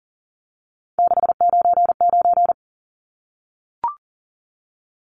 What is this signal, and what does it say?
Morse code, heard as a keyed tone, about 40 wpm, 700 Hz. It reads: 699
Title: Numbers - 3-Digit 40wpm